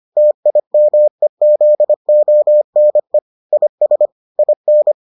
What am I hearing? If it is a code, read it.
Morse code, heard as a keyed tone, about 25 wpm, 600 Hz. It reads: TIMEZONE IS IN